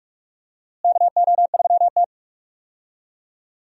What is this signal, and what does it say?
Morse code, heard as a keyed tone, about 46 wpm, 700 Hz. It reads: KO3T
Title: USA Call Signs - 4 Characters - Repeated 2x - 45wpm